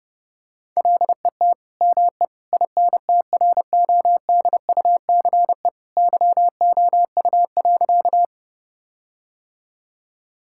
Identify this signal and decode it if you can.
Morse code, heard as a keyed tone, about 30 wpm, 700 Hz. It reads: LET ME INTRODUCE YOU.
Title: Sentences from Top 1600 Words 30wpm